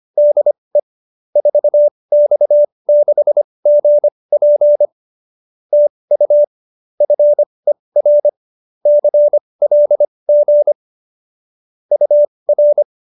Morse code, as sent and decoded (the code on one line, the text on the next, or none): -.. / ....- -..- -.... --. .--. / - ..- / ..-. .-. / -.-. .-.. --. / ..- .-.
DE 4X6GP TU FER CLG UR